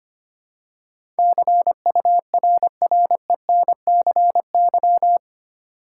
Morse code, heard as a keyed tone, about 25 wpm, 700 Hz. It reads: CURRENCY
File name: 8 Letter Words - First 3 Letters Given - 25wpm